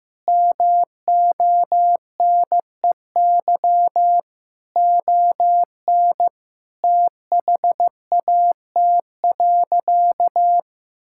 Morse code, heard as a keyed tone, about 15 wpm, 700 Hz. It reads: MONEY ON THAT.
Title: Sentences from Top 1000 Words - Review Part 5 - 15wpm